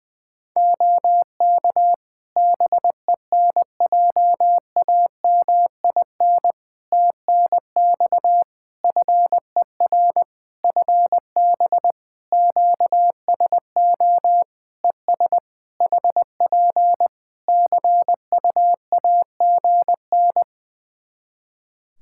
Morse code, heard as a keyed tone, about 21 wpm, 700 Hz. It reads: OK BENJAMIN TNX FER FB QSO ES HP CUAGN
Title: Practice QSO - Exchange 4 - Ending - 20wpm